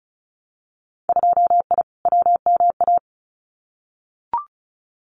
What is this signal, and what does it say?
Morse code, heard as a keyed tone, about 34 wpm, 700 Hz. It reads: 2I WMA
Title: Winter Field Day Exchanges Encore 35wpm